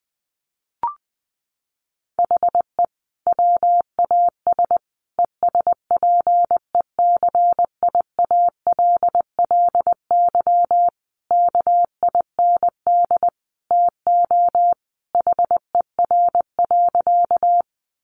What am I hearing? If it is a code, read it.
Morse code, heard as a keyed tone, about 20 wpm, 700 Hz. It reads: HE WAS ESPECIALLY KIND TO HER.
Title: Sentences from Top 700 Words 20wpm